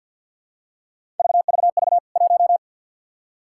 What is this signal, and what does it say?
Morse code, heard as a keyed tone, about 52 wpm, 700 Hz. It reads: VVV 1
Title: K1USN SST QSO Practice 50wpm